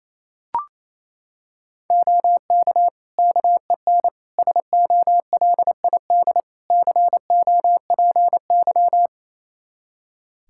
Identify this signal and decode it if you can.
Morse code, heard as a keyed tone, about 28 wpm, 700 Hz. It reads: OK KEN SOLID COPY
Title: Practice QSO - Exchange 3 - More Info - 28wpm